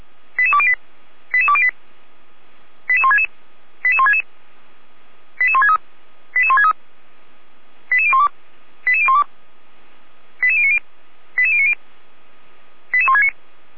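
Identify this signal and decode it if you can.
Five-tone selective call (ZVEI), heard as quick runs of five tones, tones of about 70 ms. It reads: 88208 88208 88166 88166 88163 88163 88012 88012 88090 88090 88169
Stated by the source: ZVEI five-tone selective call 88303 received by SDR.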